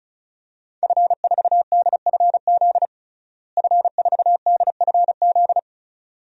Morse code, heard as a keyed tone, about 34 wpm, 700 Hz. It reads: F4DFZ F4DFZ
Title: DXCC-100 Call Signs - 5 Characters - Repeated 2x - 35wpm